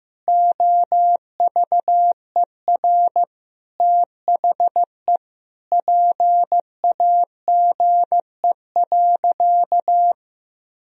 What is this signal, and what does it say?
Morse code, heard as a keyed tone, about 15 wpm, 700 Hz. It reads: OVER THE PAGE.
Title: Sentences from Top 1000 Words - Review Part 1 - 15wpm